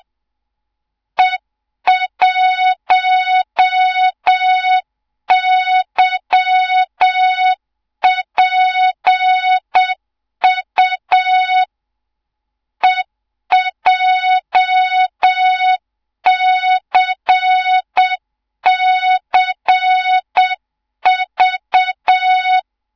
Morse code, heard as a keyed tone, about 7 wpm, 3700 Hz, with noise. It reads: E1YPU EJCCV
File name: Morze-HA2MN / hang / 24 GYAKORLAT